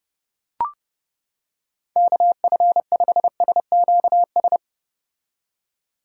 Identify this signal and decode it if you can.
Morse code, heard as a keyed tone, about 30 wpm, 700 Hz. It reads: KF5SQS